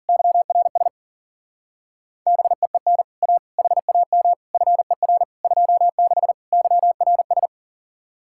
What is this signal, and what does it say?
Morse code, heard as a keyed tone, about 51 wpm, 700 Hz. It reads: YRS BEEN A HAM FER 26 YRS